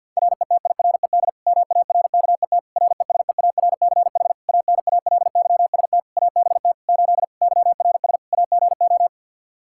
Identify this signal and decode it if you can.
Morse code, heard as a keyed tone, about 50 wpm, 700 Hz. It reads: RETIRED MARKET RESEARCH ANALYST ABT 8 YRS AGO
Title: Practice QSO - Exchange 3 - More Info - 50wpm